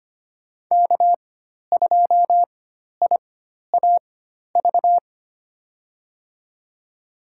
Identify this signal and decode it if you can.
Morse code, heard as a keyed tone, about 25 wpm, spaced as for 12 wpm, 700 Hz. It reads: K2IAV
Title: USA Call Signs - 5 Characters - 4x Character Spacing - 25wpm